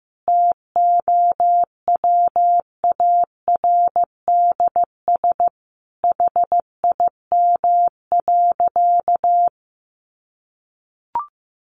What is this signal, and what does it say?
Morse code, heard as a keyed tone, about 15 wpm, 700 Hz. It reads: TOWARDS HIM.
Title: Sentences from Top 1100 Words 15wpm